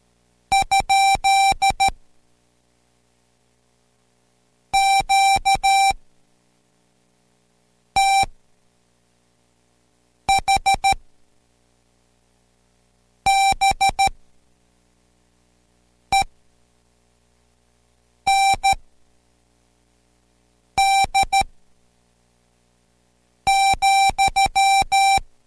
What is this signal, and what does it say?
Morse code, heard as a keyed tone, about 13 wpm, spaced as for 4 wpm, 780 Hz. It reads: ? QTH BEND,